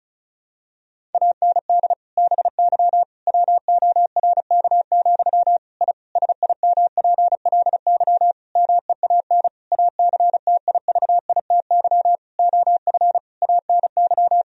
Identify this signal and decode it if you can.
Morse code, heard as a keyed tone, about 34 wpm, 700 Hz. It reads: AND BY WORK, I SIMPLY MEAN ACTIVITY OF ANY
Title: Sentences from Top 1000 Words - Review Part 4 - 35wpm